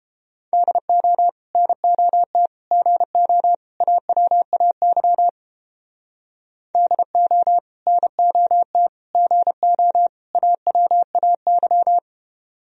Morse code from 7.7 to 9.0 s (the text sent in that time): NOT